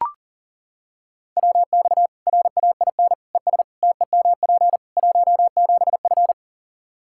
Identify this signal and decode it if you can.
Morse code, heard as a keyed tone, about 40 wpm, 700 Hz. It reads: WX RAIN ES TEMP 17F
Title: Practice QSO - Exchange 2 - Info - 40wpm